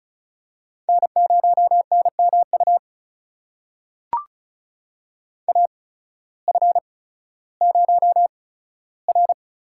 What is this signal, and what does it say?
Morse code, heard as a keyed tone, about 36 wpm, spaced as for 16 wpm, 700 Hz. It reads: N0NMU AF0R
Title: USA Call Signs - 8x Character Spacing - 35wpm